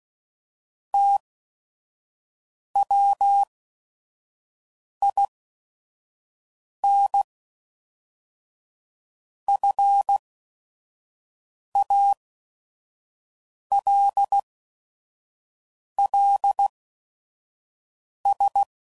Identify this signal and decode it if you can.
Morse code, heard as a keyed tone, about 16 wpm, spaced as for 5 wpm, 800 Hz. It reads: TWINFALLS
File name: CW Code Groups @5WPM / fromSM's / Cities